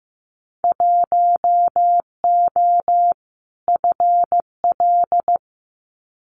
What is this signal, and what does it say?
Morse code, heard as a keyed tone, about 15 wpm, 700 Hz. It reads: 1O FL
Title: Winter Field Day Exchanges 15wpm